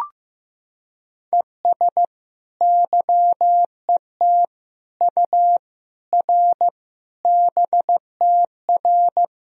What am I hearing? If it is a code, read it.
Morse code, heard as a keyed tone, about 15 wpm, 700 Hz. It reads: ES YET U R BTR
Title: Sentences from Top 500 Words and Abbreviations 15wpm